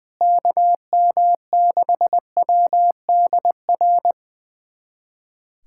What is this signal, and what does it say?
Morse code, heard as a keyed tone, about 20 wpm, 700 Hz. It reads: KM6WDR
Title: USA Callsign DE Callsign - 6 Characters - 20wpm